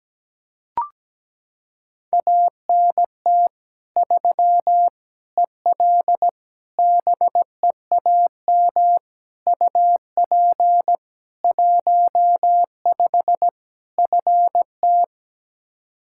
Morse code, heard as a keyed tone, about 17 wpm, 700 Hz. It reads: ANT 3 EL BEAM UP 15 FT
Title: K1USN SST QSO Practice 17wpm